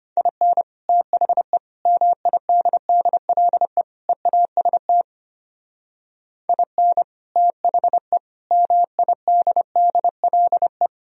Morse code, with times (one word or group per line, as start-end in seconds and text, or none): 0.2-0.6 IN
0.9-1.6 THE
1.9-3.8 MIDDLE
4.1-5.0 EAST
6.5-7.0 IN
7.4-8.2 THE
8.5-10.9 MIDDLE